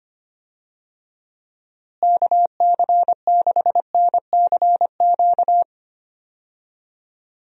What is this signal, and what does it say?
Morse code, heard as a keyed tone, about 25 wpm, 700 Hz. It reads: KC6NCQ
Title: USA Call Signs - 6 Characters - Repeated 2x - 25wpm